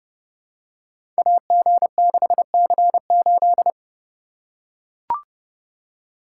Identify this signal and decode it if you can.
Morse code, heard as a keyed tone, about 32 wpm, 700 Hz. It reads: AG6C8
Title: Code Groups - 5 Letter:Numbers - 30wpm